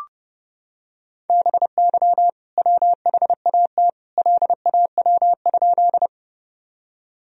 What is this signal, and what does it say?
Morse code, heard as a keyed tone, about 30 wpm, 700 Hz. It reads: BY WHAT LAW?
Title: Sentences from Top 1000 Words - Review Part 3 - 30wpm